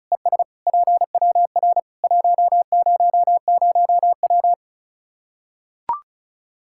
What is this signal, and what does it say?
Morse code, heard as a keyed tone, about 34 wpm, 700 Hz. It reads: ES PWR 100W
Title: Practice QSO - Exchange 2 - Info - 35wpm